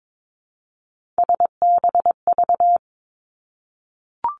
Morse code, heard as a keyed tone, about 22 wpm, 700 Hz. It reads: SBV
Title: Three Challenging Letter-Number Combinations 22wpm